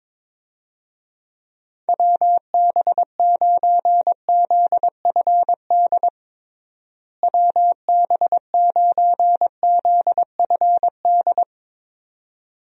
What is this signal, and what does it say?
Morse code, heard as a keyed tone, about 22 wpm, 700 Hz. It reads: WB9ZFD WB9ZFD
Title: USA Call Signs - 6 Characters - Repeated 2x - 22wpm